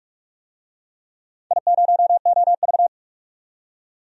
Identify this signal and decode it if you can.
Morse code, heard as a keyed tone, about 42 wpm, 700 Hz. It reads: I0OV
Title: DXCC-100 Call Signs - 4 Characters - 45wpm